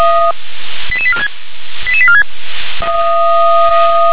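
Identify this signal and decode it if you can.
Five-tone selective call (ZVEI), heard as quick runs of five tones, tones of about 70 ms. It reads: 88846 88846
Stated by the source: ZVEI five-tone selective call received with an SDR.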